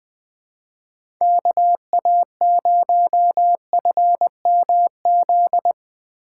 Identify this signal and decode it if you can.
Morse code, heard as a keyed tone, about 20 wpm, 700 Hz. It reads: KA0FMZ
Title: USA Call Signs - 6 Characters - 3x Character Spacing - 20wpm